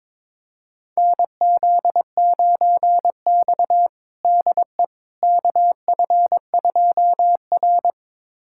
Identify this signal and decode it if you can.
Morse code, heard as a keyed tone, about 22 wpm, 700 Hz. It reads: NZ9X DE KF2R
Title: USA Callsign DE Callsign - 4 Characters - 22wpm